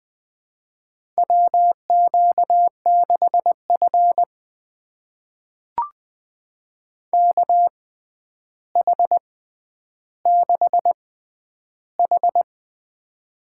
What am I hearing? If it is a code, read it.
Morse code, heard as a keyed tone, about 20 wpm, spaced as for 10 wpm, 700 Hz. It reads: WQ6F KH6H
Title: USA Call Signs - 4 Characters - 6x Character Spacing - 20wpm